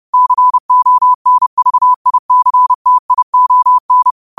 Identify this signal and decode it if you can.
Morse code, heard as a keyed tone, about 30 wpm, 1000 Hz. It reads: CONVICTION